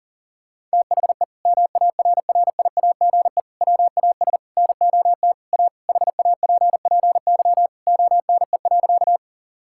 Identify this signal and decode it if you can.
Morse code, heard as a keyed tone, about 40 wpm, 700 Hz. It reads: THE MARRIAGE WAS NOT A HAPPY ONE.
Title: Sentences from Top 1000 Words - Review Part 5 - 40wpm